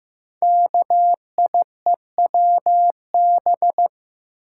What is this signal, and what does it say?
Morse code, heard as a keyed tone, about 15 wpm, 700 Hz. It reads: KIEWB
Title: Code Groups - 5 Letters - 15wpm